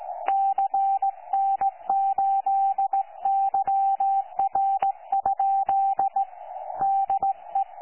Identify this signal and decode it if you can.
Morse code, heard as a keyed tone, about 16 wpm, 780 Hz. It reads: CN8YR? DE
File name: 12 45Z 18088.1kHz